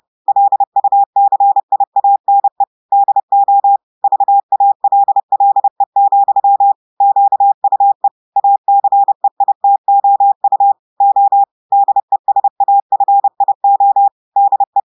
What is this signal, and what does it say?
Morse code, heard as a keyed tone, about 30 wpm, 800 Hz. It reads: LUCIANE DO VALLE, QUE ACEITOU O DESAFIO DE
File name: audios / portugais / portugais texte en clair 1 vitesse 30 mots minutes